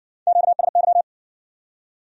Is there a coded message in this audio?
Yes: Morse code from 0.3 to 1.0 s, heard as a keyed tone, about 60 wpm, 700 Hz. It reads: QSY